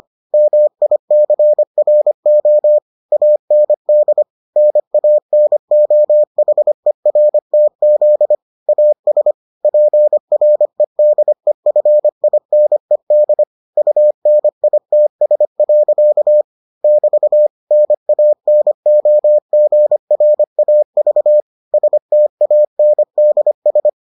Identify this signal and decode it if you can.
Morse code, heard as a keyed tone, about 25 wpm, 600 Hz. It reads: MICRO AND NANOHERTZ AS PREDEFINED UNITS. <BT> NANOGRAV STANDS